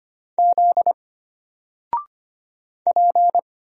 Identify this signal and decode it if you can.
Morse code, heard as a keyed tone, about 25 wpm, 700 Hz. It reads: Z P
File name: Single-Letter-Number-25wpm